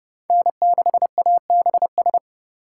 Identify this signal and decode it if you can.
Morse code, heard as a keyed tone, about 28 wpm, 700 Hz. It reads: N6ABS